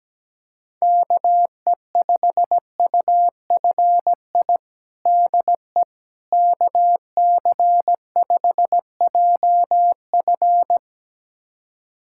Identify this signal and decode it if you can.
Morse code, heard as a keyed tone, about 17 wpm, 700 Hz. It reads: KE5UFI DE KC5JF
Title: USA Callsign DE Callsign 17wpm